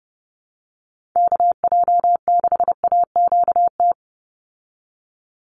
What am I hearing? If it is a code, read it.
Morse code, heard as a keyed tone, about 30 wpm, 700 Hz. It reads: KJ6AQT
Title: USA Call Signs 30wpm